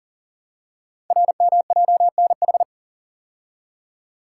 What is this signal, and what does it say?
Morse code, heard as a keyed tone, about 41 wpm, 700 Hz. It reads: RMJNH
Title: Code Groups - 5 Letters - 40wpm